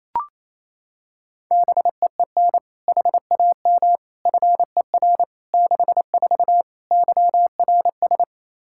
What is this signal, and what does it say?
Morse code, heard as a keyed tone, about 28 wpm, 700 Hz. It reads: BEEN HAM FER 64 YRS
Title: K1USN SST QSO Practice 28wpm